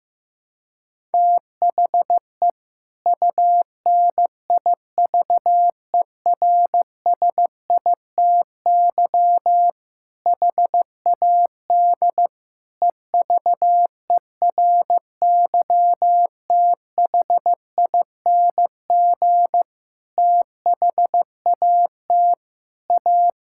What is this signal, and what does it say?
Morse code, heard as a keyed tone, about 15 wpm, 700 Hz. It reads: THE UNIVERSITY HAD EVERYTHING THAT A
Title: Sentences from Top 1000 Words - Review Part 5 - 15wpm